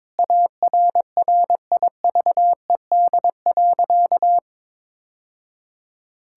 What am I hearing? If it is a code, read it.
Morse code, heard as a keyed tone, about 22 wpm, 700 Hz. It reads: ARRIVED.
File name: practice / 22wpm / Sentences with 3-Word Phrases - 22wpm